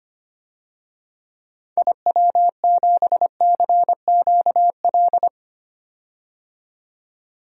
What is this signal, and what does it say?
Morse code, heard as a keyed tone, about 25 wpm, 700 Hz. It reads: IW7CQL
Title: DXCC-100 Call Signs - 6 Characters - 25wpm